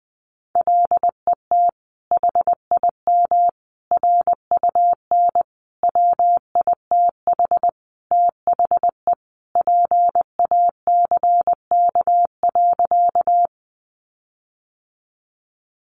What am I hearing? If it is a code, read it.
Morse code, heard as a keyed tone, about 20 wpm, 700 Hz. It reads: LET HIM RUN WITH THE PACK.
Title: Sentences from Top 1300 Words 20wpm